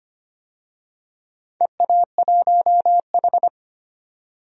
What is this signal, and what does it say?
Morse code, heard as a keyed tone, about 25 wpm, 700 Hz. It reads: EA1H
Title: DXCC-100 Call Signs - 4 Characters - 25wpm